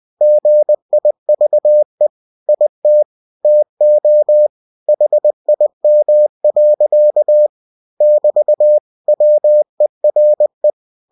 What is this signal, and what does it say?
Morse code, heard as a keyed tone, about 20 wpm, 600 Hz. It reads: GIVE IT TO HIM. <BT> WERE